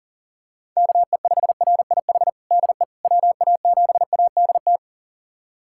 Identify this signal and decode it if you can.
Morse code, heard as a keyed tone, about 37 wpm, 700 Hz. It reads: KE5RIH DE WA7ADT
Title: USA Callsign DE Callsign - 6 Characters - 40wpm